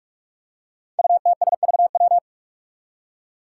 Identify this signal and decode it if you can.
Morse code, heard as a keyed tone, about 46 wpm, 700 Hz. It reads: UTSVW
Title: Five Challenging Letter-Number Combinations 45wpm